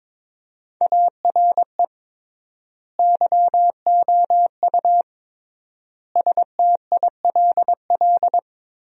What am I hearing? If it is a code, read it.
Morse code, heard as a keyed tone, about 22 wpm, 700 Hz. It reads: ARE YOU STILL
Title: 3x Word-Spacing - Sentences from Top 400 Words - 22wpm